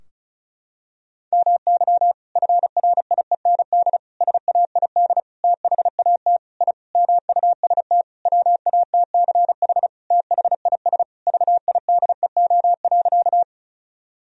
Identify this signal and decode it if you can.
Morse code, heard as a keyed tone, about 36 wpm, 700 Hz. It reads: MY FRIEND SAID THAT I MUST WATCH THIS VIDEO.
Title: Sentences from Top 1000 Words - Review Part 4 - 35wpm